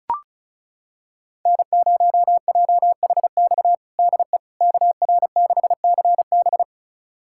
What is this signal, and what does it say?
Morse code, heard as a keyed tone, about 36 wpm, 700 Hz. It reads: N0JHX DE KR6CB
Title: USA Callsign DE Callsign - 5 Characters - 35wpm